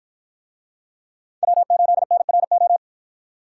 Morse code, heard as a keyed tone, about 53 wpm, 700 Hz. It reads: W8NRO